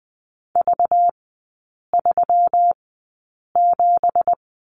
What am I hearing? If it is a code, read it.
Morse code, heard as a keyed tone, about 20 wpm, spaced as for 8 wpm, 700 Hz. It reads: V37
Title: 191 Lesson 31e - Letter v - Three Characters 20wpm